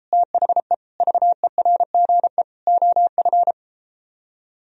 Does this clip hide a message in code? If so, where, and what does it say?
Yes: Morse code from 0.1 to 3.5 s, heard as a keyed tone, about 33 wpm, 700 Hz. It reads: THE VERGE OF